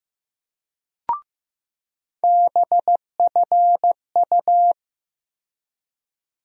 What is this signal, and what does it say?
Morse code, heard as a keyed tone, about 15 wpm, 700 Hz. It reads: BFU
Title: USA Call Sign Postfixes - 3 Letters 15wpm